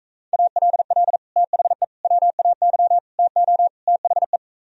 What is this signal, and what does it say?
Morse code, heard as a keyed tone, about 45 wpm, 700 Hz. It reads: ALL THE WAY TO THE